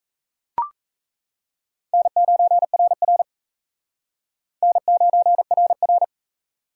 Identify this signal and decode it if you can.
Morse code, heard as a keyed tone, about 34 wpm, 700 Hz. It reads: N9RR N9RR
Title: Speed Racing - USA 1x2 Call Signs 28wpm